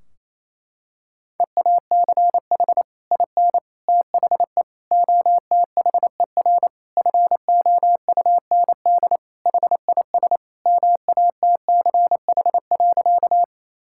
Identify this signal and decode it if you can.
Morse code, heard as a keyed tone, about 28 wpm, 700 Hz. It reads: EACH IN THE OTHER FOUND HIS MATCH.
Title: Sentences from Top 1600 Words 28wpm